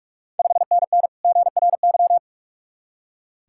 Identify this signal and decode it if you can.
Morse code, heard as a keyed tone, about 44 wpm, 690 Hz. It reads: 5NN GRY
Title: Ontario QSO Party Exchange 45wpm